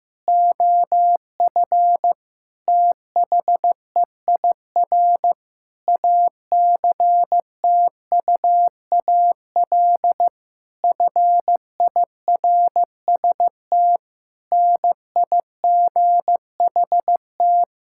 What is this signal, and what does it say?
Morse code, heard as a keyed tone, about 15 wpm, 700 Hz. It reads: OF THEIR ACTUAL FIRST NIGHT
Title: Sentences from Top 1000 Words - Review Part 5 - 15wpm